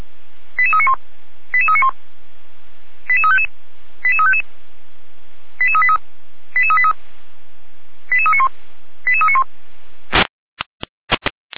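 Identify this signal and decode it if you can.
Five-tone selective call (ZVEI), heard as quick runs of five tones, tones of about 70 ms. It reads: 88391 88391 88355 88355 88383 88383 88391 88391
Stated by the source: ZVEI five-tone selective call 88335 received with an SDR.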